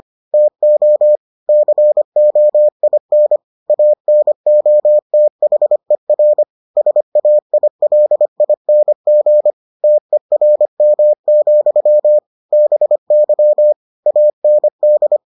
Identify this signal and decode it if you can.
Morse code, heard as a keyed tone, about 25 wpm, 600 Hz. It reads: TO COIN ANOTHER SAILING TERM, BY AND